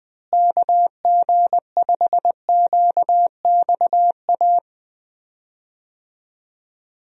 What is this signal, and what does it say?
Morse code, heard as a keyed tone, about 20 wpm, 700 Hz. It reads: KG5QXA
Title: USA Call Signs - 6 Characters - Repeated 3x - 20wpm